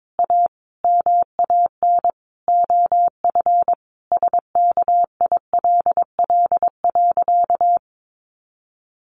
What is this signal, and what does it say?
Morse code, heard as a keyed tone, about 22 wpm, 700 Hz. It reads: A MAN OF SKILL.
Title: Sentences from Top 1200 Words 22wpm